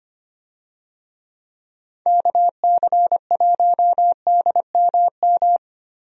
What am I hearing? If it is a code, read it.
Morse code, heard as a keyed tone, about 25 wpm, 700 Hz. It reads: KC1DMM
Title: USA Call Signs - 6 Characters - 25wpm